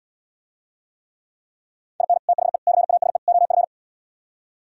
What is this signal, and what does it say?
Morse code, heard as a keyed tone, about 77 wpm, 700 Hz. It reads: IN THE CASE OF